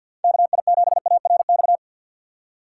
Morse code, heard as a keyed tone, about 53 wpm, 700 Hz. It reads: KI7ARX
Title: USA Call Signs - 6 Characters - 4x Character Spacing - 50wpm